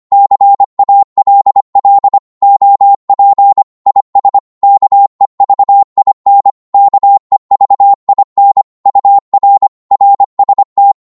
Morse code, heard as a keyed tone, about 25 wpm, 800 Hz. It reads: CALL OP IS KEVIN KEVIN UR RST
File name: qso-25wpm